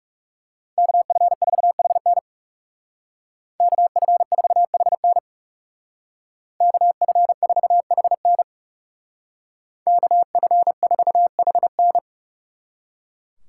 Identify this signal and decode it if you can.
Morse code, heard as a keyed tone, about 35 wpm, 700 Hz. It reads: KF4HN KF4HN KF4HN KF4HN
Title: Speed Racing - USA Call Signs 30wpm